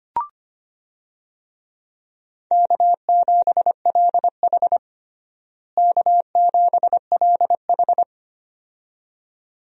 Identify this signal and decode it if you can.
Morse code, heard as a keyed tone, about 25 wpm, 700 Hz. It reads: K7LH K7LH
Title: USA Call Signs - 4 Characters - Repeated 2x - 25wpm